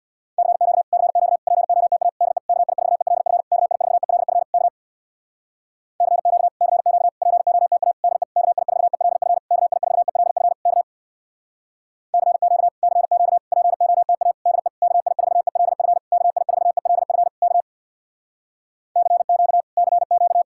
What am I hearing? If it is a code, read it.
Morse code, heard as a keyed tone, about 64 wpm, 700 Hz. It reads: CQ CQ POTA DE KI4ELF KI4ELF K CQ CQ POTA DE KI4ELF KI4ELF K CQ CQ POTA DE KI4ELF KI4ELF K CQ CQ